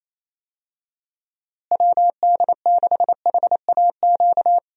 Morse code, heard as a keyed tone, about 28 wpm, 700 Hz. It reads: WD6HAQ
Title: USA Call Signs - 6 Characters - Repeated 2x - 28wpm